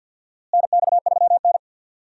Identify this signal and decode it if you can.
Morse code, heard as a keyed tone, about 52 wpm, 700 Hz. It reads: NX3N
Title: USA Call Signs - 4 Characters - Repeated 2x - 50wpm